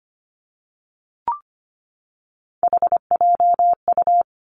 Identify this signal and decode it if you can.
Morse code, heard as a keyed tone, about 25 wpm, 700 Hz. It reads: HJU
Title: Three Challenging Letter-Number Combinations 25wpm